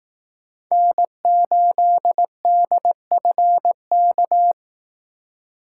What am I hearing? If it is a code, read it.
Morse code, heard as a keyed tone, about 18 wpm, 700 Hz. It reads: N8DFK
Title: Speed Racing - USA 1x3 Call Signs 15wpm